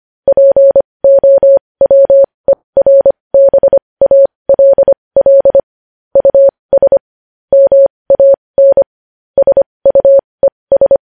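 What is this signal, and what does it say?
Morse code, heard as a keyed tone, about 25 wpm, spaced as for 20 wpm, 550 Hz. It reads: POWERBALL US MAN SUES
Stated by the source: Morse news headlines practice at 20 wpm.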